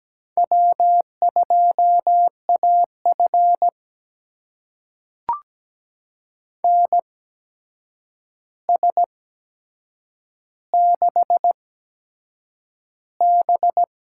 Morse code, heard as a keyed tone, about 17 wpm, spaced as for 7 wpm, 700 Hz. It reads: W2AF NS6B
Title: USA Call Signs - 4 Characters - 8x Character Spacing - 17wpm